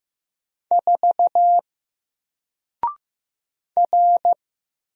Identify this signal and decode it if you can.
Morse code, heard as a keyed tone, about 15 wpm, 700 Hz. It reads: 4 R
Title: Single Character (Letter:Number:Punctuation:Prosign) 15wpm